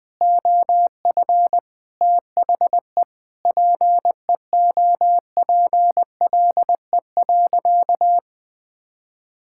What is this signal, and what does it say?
Morse code, heard as a keyed tone, about 20 wpm, 700 Hz. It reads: OF THE PEOPLE.
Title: Sentences from Top 1600 Words 20wpm